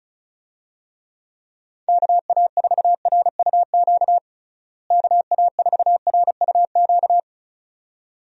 Morse code, heard as a keyed tone, about 34 wpm, 700 Hz. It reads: KA4RUQ KA4RUQ
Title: USA Call Signs - 6 Characters - Repeated 2x - 35wpm